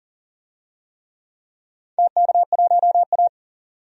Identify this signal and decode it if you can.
Morse code, heard as a keyed tone, about 40 wpm, 700 Hz. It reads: TK1A
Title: DXCC-100 Call Signs - 4 Characters - 40wpm